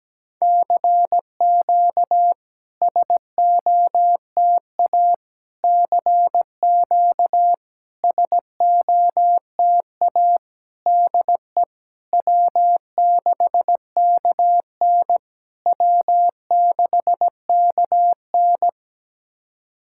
Morse code, heard as a keyed tone, about 16 wpm, 700 Hz. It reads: CQ SOTA CQ SOTA DE W6KN W6KN
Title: SOTA QSO Practice 17wpm